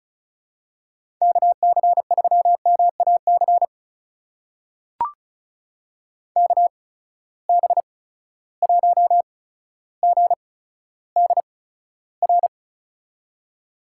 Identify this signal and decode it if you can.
Morse code, heard as a keyed tone, about 36 wpm, spaced as for 15 wpm, 700 Hz. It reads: KC3MAC KB1GDR
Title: USA Call Signs - 8x Character Spacing - 35wpm